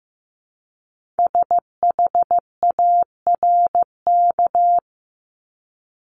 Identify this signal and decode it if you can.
Morse code, heard as a keyed tone, about 15 wpm, 700 Hz. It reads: SHARK
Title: Multi-letter Phonemes 15wpm